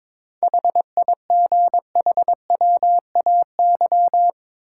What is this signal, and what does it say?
Morse code, heard as a keyed tone, about 22 wpm, 700 Hz. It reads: HIGHWAY
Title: Sentences from Top 1400 Words - New Word Drill - 22wpm